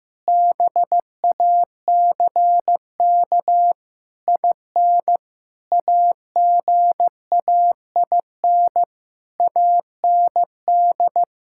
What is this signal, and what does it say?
Morse code, heard as a keyed tone, about 15 wpm, 700 Hz. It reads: BACK IN AGAIN AND
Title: Pangram Challenge v2 - 15wpm